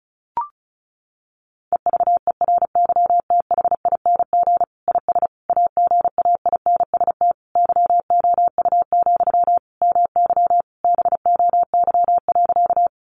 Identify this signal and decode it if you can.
Morse code, heard as a keyed tone, about 34 wpm, 700 Hz. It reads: EVERYTHING IS AGAINST YOU, MY BOY.
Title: Sentences from Top 700 Words 35wpm